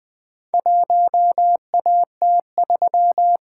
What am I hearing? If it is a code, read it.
Morse code, heard as a keyed tone, about 21 wpm, 700 Hz. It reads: 1AT3
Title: Four Letter-Number Combinations - 20wpm